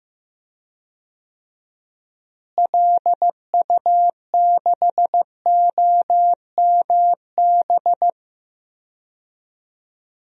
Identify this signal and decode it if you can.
Morse code, heard as a keyed tone, about 15 wpm, 700 Hz. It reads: LU6OMB
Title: DXCC-100 Call Signs - 6 Characters - Repeated 3x - 15wpm